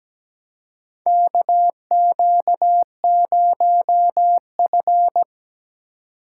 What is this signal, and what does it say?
Morse code, heard as a keyed tone, about 17 wpm, 700 Hz. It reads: KQ0F
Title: USA Call Signs - 4 Characters - 4x Character Spacing - 17wpm